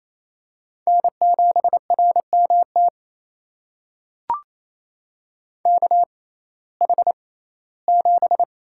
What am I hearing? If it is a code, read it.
Morse code, heard as a keyed tone, about 28 wpm, spaced as for 17 wpm, 700 Hz. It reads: N7RMT KH7
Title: USA Call Signs - 6x Character Spacing - 28wpm